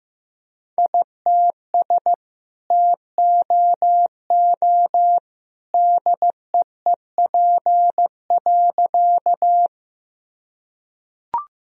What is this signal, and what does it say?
Morse code, heard as a keyed tone, about 15 wpm, 700 Hz. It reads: ITS TOO DEEP.
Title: Sentences from Top 1000 Words - Review Part 3 - 15wpm